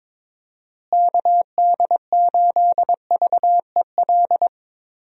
Morse code, heard as a keyed tone, about 22 wpm, 700 Hz. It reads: KD8VEL
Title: USA Call Signs - 6 Characters - Repeated 3x - 22wpm